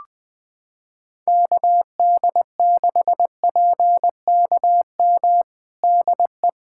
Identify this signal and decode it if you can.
Morse code, heard as a keyed tone, about 20 wpm, 700 Hz. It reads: KD6PKM DE
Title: USA Callsign DE Callsign 20wpm